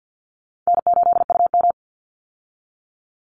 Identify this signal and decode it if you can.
Morse code, heard as a keyed tone, about 48 wpm, 700 Hz. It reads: N8UM
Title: Easy Call Signs Repeated 3x - 50wpm